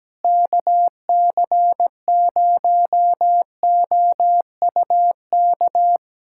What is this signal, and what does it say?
Morse code, heard as a keyed tone, about 17 wpm, 700 Hz. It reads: KC0OUK
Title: USA Callsign DE Callsign - 6 Characters - 17wpm